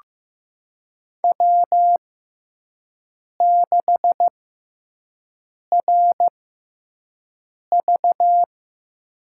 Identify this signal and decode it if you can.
Morse code, heard as a keyed tone, about 15 wpm, spaced as for 5 wpm, 700 Hz. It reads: W6RV
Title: USA Call Signs - 4 Characters - 6x Character Spacing - 15wpm